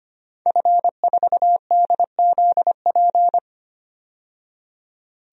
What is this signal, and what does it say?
Morse code, heard as a keyed tone, about 25 wpm, 700 Hz. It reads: F4DZP
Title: DXCC-100 Call Signs - 5 Characters - 25wpm